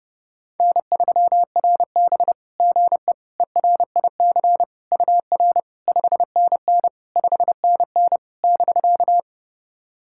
Morse code, heard as a keyed tone, about 30 wpm, 700 Hz. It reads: N3RB GE ERIC UR 5NN 5NN <BK>
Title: SOTA QSO Practice 30wpm